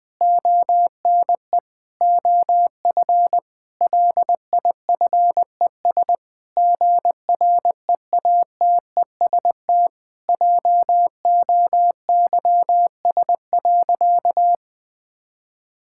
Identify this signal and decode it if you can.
Morse code, heard as a keyed tone, about 20 wpm, 700 Hz. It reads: ONE OF LIFES GREATEST JOYS.
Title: Difficult Sentences with Hints - 20wpm